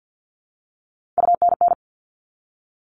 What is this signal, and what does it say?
Morse code, heard as a keyed tone, about 50 wpm, 700 Hz. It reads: UNN
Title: All Cut Numbers - 3-Digit 50wpm